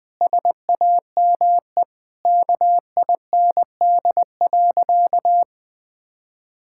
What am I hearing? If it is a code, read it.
Morse code, heard as a keyed tone, about 21 wpm, 700 Hz. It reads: SAME KIND.
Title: Sentences with 3-Word Phrases - 20wpm